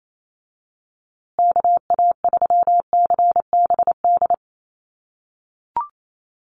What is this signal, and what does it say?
Morse code, heard as a keyed tone, about 28 wpm, 700 Hz. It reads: KA3CBD
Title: USA Call Signs Repeated 3x 28wpm